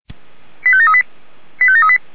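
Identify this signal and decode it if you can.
Five-tone selective call (ZVEI), heard as quick runs of five tones, tones of about 70 ms. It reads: 85738 85738
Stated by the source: ZVEI five-tone selective call received on SDR.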